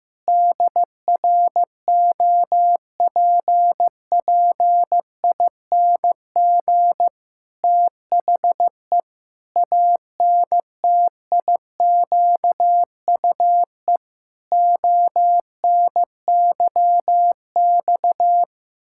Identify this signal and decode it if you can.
Morse code, heard as a keyed tone, about 15 wpm, 700 Hz. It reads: DROPPING THE ANTIQUE ONYX
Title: Pangram Challenge v2 - 15wpm